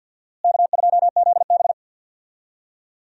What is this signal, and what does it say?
Morse code, heard as a keyed tone, about 46 wpm, 700 Hz. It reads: K2ZB
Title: USA Call Signs - 4 Characters - Repeated 3x - 50wpm